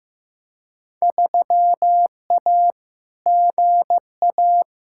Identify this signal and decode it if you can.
Morse code, heard as a keyed tone, about 15 wpm, 700 Hz. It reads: 3A GA
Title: Field Day Exchanges Encore - 15wpm